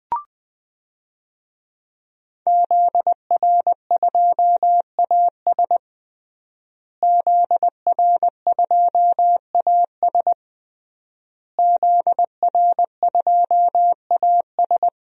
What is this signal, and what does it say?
Morse code, heard as a keyed tone, about 20 wpm, 700 Hz. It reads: ZR2AS ZR2AS ZR2AS
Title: DXCC-100 Call Signs - 5 Characters - Repeated 3x - 20wpm